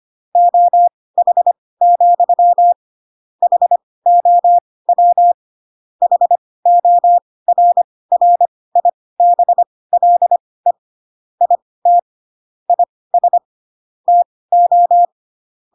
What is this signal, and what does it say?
Morse code, heard as a keyed tone, about 25 wpm, spaced as for 18 wpm, 700 Hz. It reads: OH, HOW HORRIBLE IT IS TO